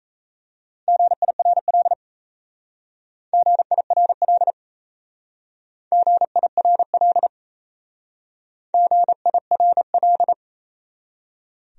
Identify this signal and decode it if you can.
Morse code, heard as a keyed tone, about 37 wpm, 690 Hz. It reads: GIRL GIRL GIRL GIRL